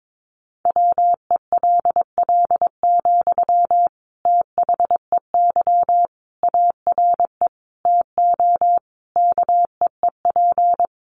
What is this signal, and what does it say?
Morse code, heard as a keyed tone, about 22 wpm, 700 Hz. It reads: WELL, THEY ARE TO KEEP
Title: Sentences from Top 1500 Words 22wpm